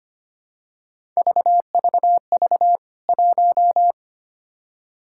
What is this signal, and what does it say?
Morse code, heard as a keyed tone, about 25 wpm, 700 Hz. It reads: VVV 1
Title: K1USN SST QSO Practice 25wpm